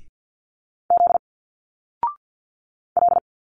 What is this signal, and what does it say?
Morse code, heard as a keyed tone, about 51 wpm, 700 Hz. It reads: Z L
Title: Single Letters 50wpm